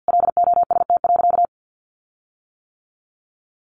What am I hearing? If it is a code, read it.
Morse code, heard as a keyed tone, about 50 wpm, 700 Hz. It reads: LOST.
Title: Sentences from Top 1400 Words 50wpm